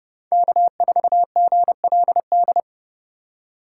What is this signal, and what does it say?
Morse code, heard as a keyed tone, about 35 wpm, 700 Hz. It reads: K4GLD
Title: Speed Racing - USA 1x3 Call Signs 30wpm